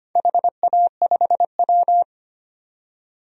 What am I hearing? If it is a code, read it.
Morse code, heard as a keyed tone, about 25 wpm, 700 Hz. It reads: HA5W
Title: DXCC-100 Call Signs - 4 Characters - 25wpm